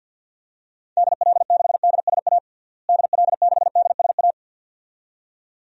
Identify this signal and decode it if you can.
Morse code, heard as a keyed tone, about 52 wpm, 700 Hz. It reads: DL6DSA DL6DSA
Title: DXCC-100 Call Signs - 6 Characters - Repeated 2x - 50wpm